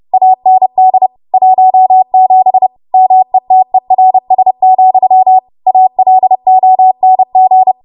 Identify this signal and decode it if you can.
Morse code, heard as a keyed tone, about 30 wpm, 750 Hz. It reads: AND 17 METERS, ALONG